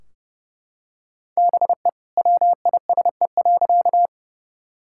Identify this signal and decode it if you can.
Morse code, heard as a keyed tone, about 30 wpm, 700 Hz. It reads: BE WISE.
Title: Sentences from Top 1600 Words 30wpm